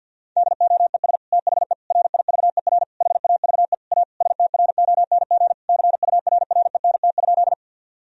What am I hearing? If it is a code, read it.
Morse code, heard as a keyed tone, about 50 wpm, 700 Hz. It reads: DOES THE RIVER HAVE A STRONG CURRENT?
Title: Sentences from Top 1000 Words - Review Part 4 - 50wpm